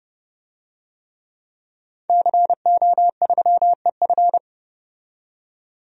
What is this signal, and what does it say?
Morse code, heard as a keyed tone, about 30 wpm, 700 Hz. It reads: CO3EF
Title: DXCC-100 Call Signs - 5 Characters - 30wpm